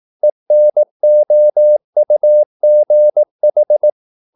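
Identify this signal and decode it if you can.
Morse code, heard as a keyed tone, about 18 wpm, 600 Hz. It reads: ENOUGH
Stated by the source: Morse practice at 18 wpm, English text, from CW med Gnister.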